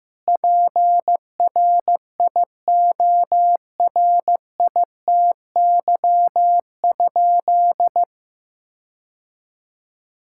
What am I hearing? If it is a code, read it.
Morse code, heard as a keyed tone, about 15 wpm, 700 Hz. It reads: PRIORITY?
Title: Sentences from Top 1600 Words 15wpm